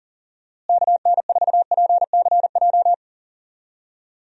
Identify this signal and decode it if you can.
Morse code, heard as a keyed tone, about 40 wpm, 700 Hz. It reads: KN4PCJ
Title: USA Call Signs - 6 Characters - Repeated 3x - 40wpm